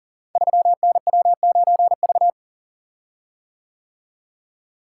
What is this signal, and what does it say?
Morse code, heard as a keyed tone, about 38 wpm, 700 Hz. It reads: 3NW9V
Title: Code Groups - 5 Letter:Numbers - 40wpm